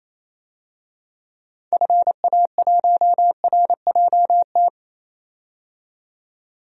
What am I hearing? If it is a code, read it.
Morse code, heard as a keyed tone, about 28 wpm, 700 Hz. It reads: FA1RJT
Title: DXCC-100 Call Signs - 6 Characters - 28wpm